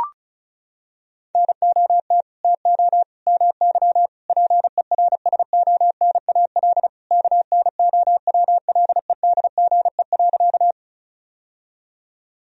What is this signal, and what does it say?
Morse code, heard as a keyed tone, about 34 wpm, 700 Hz. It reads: NOT TO MY PERSONAL KNOWLEDGE.
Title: Sentences from Top 1000 Words - Review Part 4 - 35wpm